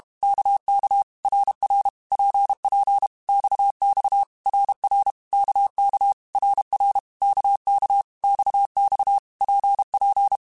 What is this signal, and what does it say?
Morse code, heard as a keyed tone, about 32 wpm, 800 Hz. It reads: KK RR PP XX RR KK RR KK XX PP
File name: Basic Letter Groups By7Groups / RKPX / RKPX@30